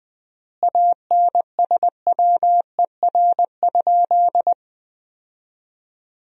Morse code, heard as a keyed tone, about 20 wpm, 700 Hz. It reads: ANSWER?
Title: 3x Word-Spacing - Sentences from Top 300 Words - 20wpm